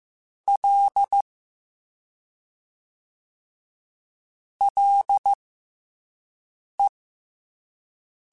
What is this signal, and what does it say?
Morse code, heard as a keyed tone, about 15 wpm, spaced as for 5 wpm, 800 Hz. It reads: L LE
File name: F and L / F and L mp3 / F and L Prefixes 3x2 Twice @5